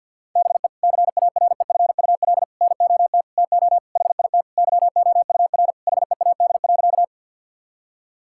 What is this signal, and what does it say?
Morse code, heard as a keyed tone, about 50 wpm, 700 Hz. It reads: BE CAREFUL NOT TO HIT YOUR HEAD.
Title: Sentences from Top 1600 Words 50wpm